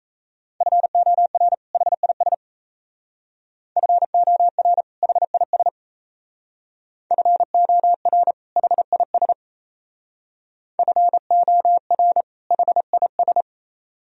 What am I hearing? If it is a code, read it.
Morse code, heard as a keyed tone, about 36 wpm, 700 Hz. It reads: FOR HIS FOR HIS FOR HIS FOR HIS